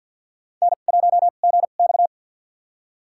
Morse code, heard as a keyed tone, about 52 wpm, spaced as for 37 wpm, 700 Hz. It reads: N1GX
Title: USA Call Signs - 4 Characters - 2x Character Spacing - 50wpm